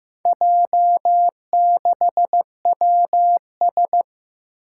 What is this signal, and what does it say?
Morse code, heard as a keyed tone, about 15 wpm, 700 Hz. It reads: J6WS
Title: Four Challenging Letter-Number Combinations 15wpm